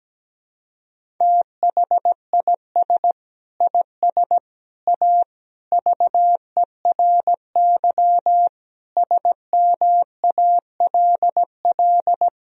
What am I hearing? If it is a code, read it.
Morse code, heard as a keyed tone, about 17 wpm, 700 Hz. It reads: THIS IS A VERY SMALL
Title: Sentences from Top 1000 Words - Review Part 1 - 17wpm